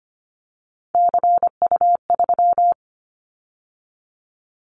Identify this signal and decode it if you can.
Morse code, heard as a keyed tone, about 25 wpm, 700 Hz. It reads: CU3
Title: Callsign Prefixes for Top 100 DXCC - 25wpm